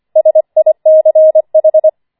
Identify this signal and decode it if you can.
Morse code, heard as a keyed tone, about 25 wpm, 630 Hz. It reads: SICH